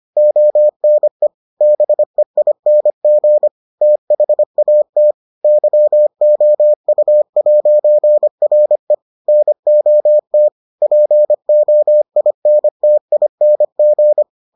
Morse code, heard as a keyed tone, about 25 wpm, 600 Hz. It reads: ONE BEING THAT YOU'RE NOT POINTING